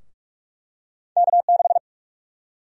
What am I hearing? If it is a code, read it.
Morse code, heard as a keyed tone, about 46 wpm, 700 Hz. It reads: K6
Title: USA Call Sign Prefixes - Repeated 2x - 45wpm